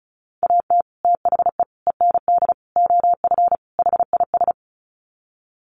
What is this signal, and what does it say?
Morse code, heard as a keyed tone, about 34 wpm, 700 Hz. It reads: AT THE END OF HIS
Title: Sets of 5 Words Encore 35wpm